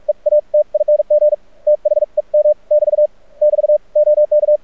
Morse code, heard as a keyed tone, about 48 wpm, spaced as for 40 wpm, 600 Hz. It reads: EATING THEM <BT> <BT> OK
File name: archived / CRSnet